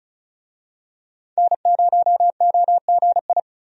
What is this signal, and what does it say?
Morse code, heard as a keyed tone, about 34 wpm, 700 Hz. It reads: N0OGI
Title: USA Call Signs - 5 Characters - 35wpm